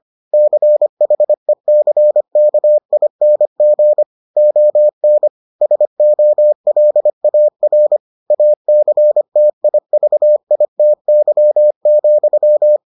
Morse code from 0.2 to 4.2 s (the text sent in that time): CHECKING